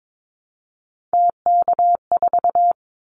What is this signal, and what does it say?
Morse code, heard as a keyed tone, about 22 wpm, 700 Hz. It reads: TK4
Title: Callsign Prefixes for Top 100 DXCC - 22wpm